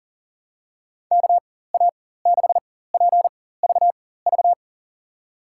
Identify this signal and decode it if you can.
Morse code, heard as a keyed tone, about 40 wpm, spaced as for 19 wpm, 700 Hz. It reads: KA6PVV
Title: USA Call Signs - 6 Characters - 4x Character Spacing - 40wpm